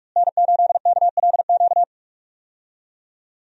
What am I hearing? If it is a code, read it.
Morse code, heard as a keyed tone, about 46 wpm, 700 Hz. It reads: N8KLQ
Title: USA Call Signs - 5 Characters - Repeated 3x - 45wpm